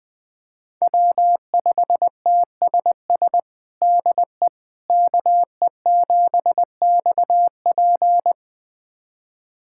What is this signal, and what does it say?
Morse code, heard as a keyed tone, about 21 wpm, 700 Hz. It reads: W5TSS DE KE7XP
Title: USA Callsign DE Callsign 20wpm